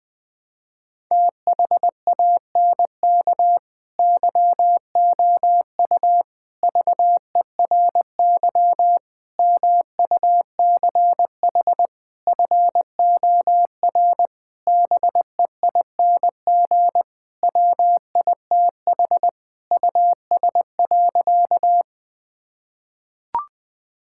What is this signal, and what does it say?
Morse code, heard as a keyed tone, about 20 wpm, 700 Hz. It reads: THANK YOU VERY MUCH FOR BEING WITH US.
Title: Sentences with 5-Word Phrases - 20wpm